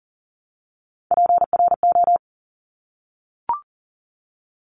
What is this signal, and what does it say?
Morse code, heard as a keyed tone, about 40 wpm, 700 Hz. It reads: PRO
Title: Top Three Letter Combinations in Words 40wpm